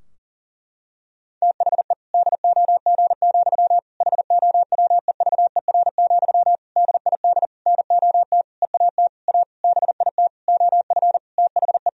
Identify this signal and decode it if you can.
Morse code, heard as a keyed tone, about 40 wpm, 700 Hz. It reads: THE DOG, HOWEVER, DID NOT EAT A BIT OF THE
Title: Sentences from Top 1000 Words - Review Part 3 - 40wpm